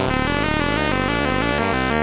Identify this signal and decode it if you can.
There is no code.